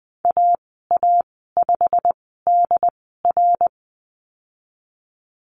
Morse code, heard as a keyed tone, about 20 wpm, spaced as for 15 wpm, 700 Hz. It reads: AA5DR
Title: 088 Lesson 15g - Number 5 - Call Signs 20wpm